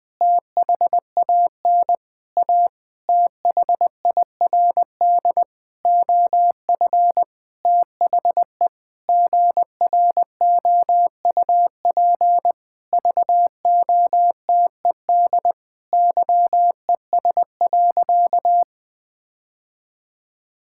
Morse code, heard as a keyed tone, about 20 wpm, 700 Hz. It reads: THAN A THIRD OF THE GROUP VOTED YES.
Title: Sentences with 6-Word Phrases - Spoken Beginning - 20wpm